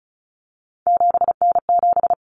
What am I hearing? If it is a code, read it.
Morse code, heard as a keyed tone, about 34 wpm, 700 Hz. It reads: 7N7
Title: Common Cut Numbers - 3-Digit 35wpm